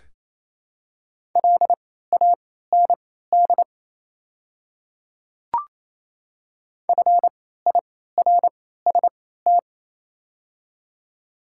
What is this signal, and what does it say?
Morse code, heard as a keyed tone, about 27 wpm, spaced as for 16 wpm, 700 Hz. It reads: LAND FIRST
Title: 3x Character Spacing - Top 200 Words - 28wpm